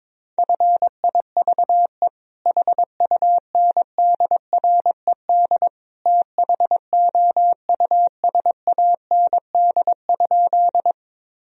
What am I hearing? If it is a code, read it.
Morse code, heard as a keyed tone, about 22 wpm, 700 Hz. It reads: FIVE HUNDRED THOUSAND?
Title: Sentences from Top 1000 Words - Review Part 3 - 22wpm